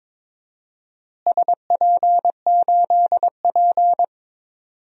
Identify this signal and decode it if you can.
Morse code, heard as a keyed tone, about 22 wpm, 700 Hz. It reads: SP8P
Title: DXCC-100 Call Signs - 4 Characters - Repeated 2x - 22wpm